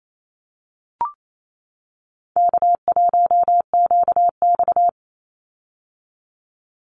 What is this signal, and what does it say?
Morse code, heard as a keyed tone, about 28 wpm, 700 Hz. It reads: K1QX
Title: Easy Call Signs - 28wpm